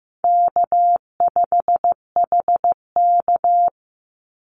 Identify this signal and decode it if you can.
Morse code, heard as a keyed tone, about 15 wpm, 700 Hz. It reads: K5HK
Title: Easy Call Signs - 15wpm